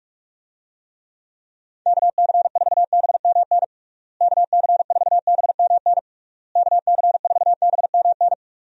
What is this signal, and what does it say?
Morse code, heard as a keyed tone, about 46 wpm, 700 Hz. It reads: KC4BMN KC4BMN KC4BMN
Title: USA Call Signs - 6 Characters - Repeated 3x - 45wpm